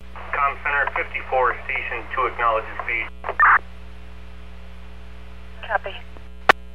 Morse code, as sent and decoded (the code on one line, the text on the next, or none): none